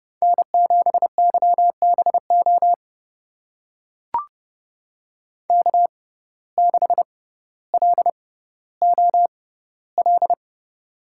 Morse code, heard as a keyed tone, about 29 wpm, spaced as for 15 wpm, 700 Hz. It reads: N7YBO K6LOL
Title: USA Call Signs - 5 Characters - 6x Character Spacing - 30wpm